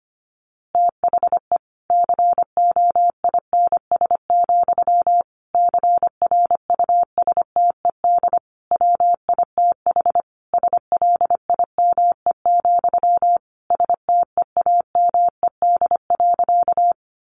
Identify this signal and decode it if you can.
Morse code, heard as a keyed tone, about 25 wpm, 700 Hz. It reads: THE COINS, CRUSTED WITH SLIME, STEAMED.
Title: A Sound Of Thunder 25wpm